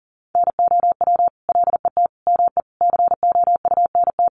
Morse code, heard as a keyed tone, about 40 wpm, 700 Hz. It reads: NOW LET ME COUNT
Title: Sentences from Top 1500 Words 40wpm